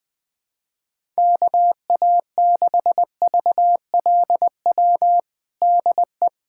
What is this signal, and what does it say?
Morse code, heard as a keyed tone, about 20 wpm, 700 Hz. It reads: KA6VLW DE
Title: USA Callsign DE Callsign - 6 Characters - 20wpm